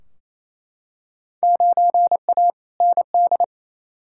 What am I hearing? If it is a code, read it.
Morse code, heard as a keyed tone, about 28 wpm, 700 Hz. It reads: 9A ND
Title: Field Day Exchanges Encore - 28wpm